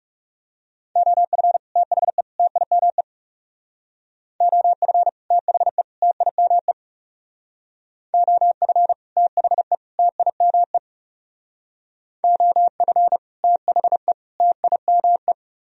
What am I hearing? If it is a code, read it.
Morse code, heard as a keyed tone, about 40 wpm, 700 Hz. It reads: OF THE TIME OF THE TIME OF THE TIME OF THE TIME